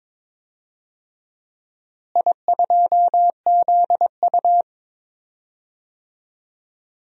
Morse code, heard as a keyed tone, about 21 wpm, 700 Hz. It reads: I2ZU
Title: DXCC-100 Call Signs - 4 Characters - Repeated 3x - 22wpm